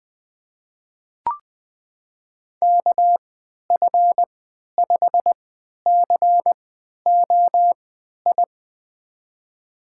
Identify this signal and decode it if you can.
Morse code, heard as a keyed tone, about 21 wpm, spaced as for 11 wpm, 700 Hz. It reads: KF5COI
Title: USA Call Signs - 6 Characters - 3x Character Spacing - 20wpm